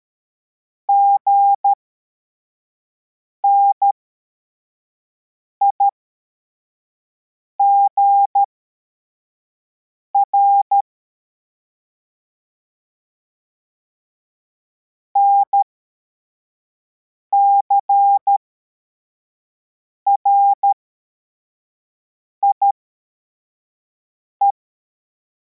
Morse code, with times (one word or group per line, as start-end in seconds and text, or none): 0.9-10.8 GNIGR
15.2-24.5 NCRIE